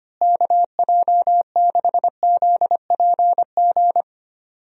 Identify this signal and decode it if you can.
Morse code, heard as a keyed tone, about 25 wpm, 700 Hz. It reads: KJ6ZPG
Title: USA Call Signs - 6 Characters - Repeated 2x - 25wpm